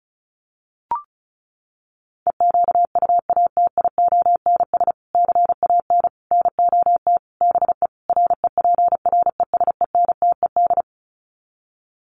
Morse code, heard as a keyed tone, about 34 wpm, 700 Hz. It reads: EQUATIONS CAN NOT BE REPRESENTED
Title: Sets of 5 Words 35wpm